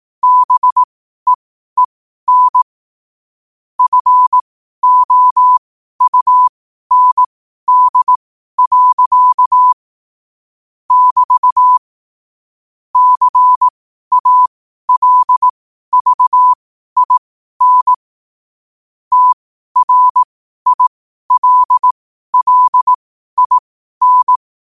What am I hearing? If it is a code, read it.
Morse code, heard as a keyed tone, about 18 wpm, spaced as for 12 wpm, 1000 Hz. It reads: BEEN FOUND. <BT> CALVIN TRILLIN
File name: qotdfiles / 12wpm